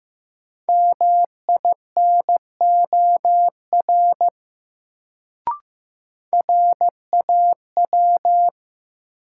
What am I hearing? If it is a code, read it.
Morse code, heard as a keyed tone, about 15 wpm, 700 Hz. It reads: MINOR RAW
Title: Sentences from Top 1500 Words - New Word Drill Alt - 15wpm